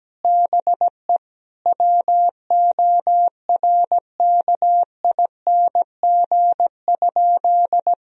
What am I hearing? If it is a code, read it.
Morse code, heard as a keyed tone, about 17 wpm, 700 Hz. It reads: BE WORKING?
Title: Sentences from Top 1000 Words - Review Part 4 - 17wpm